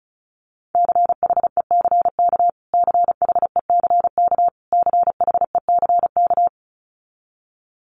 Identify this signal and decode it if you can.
Morse code, heard as a keyed tone, about 34 wpm, 700 Hz. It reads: CHECK CHECK CHECK
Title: Top 500 Words Repeated 35wpm